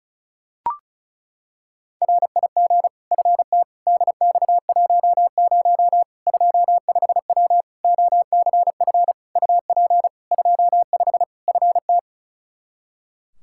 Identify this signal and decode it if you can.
Morse code, heard as a keyed tone, about 36 wpm, 700 Hz. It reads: RIG FT DX10 25W OCF UP 25 FT
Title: Practice QSO - Exchange 2 - Info - 35wpm